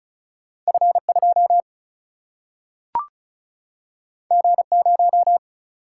Morse code, heard as a keyed tone, about 36 wpm, 700 Hz. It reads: F2 G0
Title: Single-Letter Single-Number 35wpm